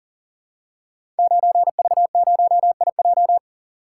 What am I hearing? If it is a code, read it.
Morse code, heard as a keyed tone, about 42 wpm, 690 Hz. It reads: 9V0IJ
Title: Code Groups - 5 Letter:Numbers - 40wpm